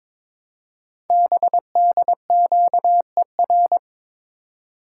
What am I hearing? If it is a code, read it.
Morse code, heard as a keyed tone, about 22 wpm, 700 Hz. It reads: BDQER
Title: Code Groups - 5 Letters - 22wpm